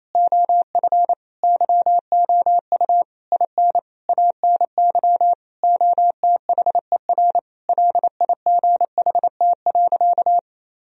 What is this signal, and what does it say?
Morse code, heard as a keyed tone, about 28 wpm, 700 Hz. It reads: OF YOU IN ANY OTHER LIGHT.
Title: Sentences from Top 1000 Words - Review Part 1 - 28wpm